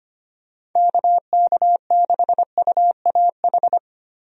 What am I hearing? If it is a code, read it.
Morse code, heard as a keyed tone, about 25 wpm, 700 Hz. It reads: KK6UAH
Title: USA Call Signs - 6 Characters - Repeated 3x - 25wpm